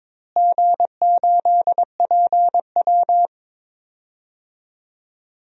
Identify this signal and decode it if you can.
Morse code, heard as a keyed tone, about 22 wpm, 700 Hz. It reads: G8PW
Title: DXCC-100 Call Signs - 4 Characters - Repeated 2x - 22wpm